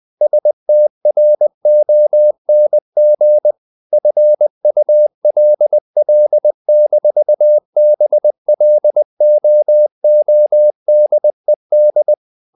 Morse code, heard as a keyed tone, about 20 wpm, 600 Hz. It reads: STRONG FULL-BLOODED
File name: the old man and the sea / the old man and the sea part 1 20wpm